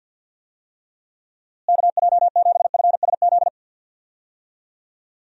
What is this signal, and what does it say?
Morse code, heard as a keyed tone, about 46 wpm, 700 Hz. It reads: KJ7FSZ
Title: USA Call Signs - 6 Characters - Repeated 2x - 50wpm